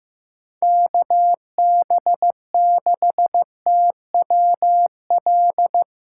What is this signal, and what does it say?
Morse code, heard as a keyed tone, about 15 wpm, 700 Hz. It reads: KB6TWL
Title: USA Call Signs - 6 Characters - 6x Character Spacing - 15wpm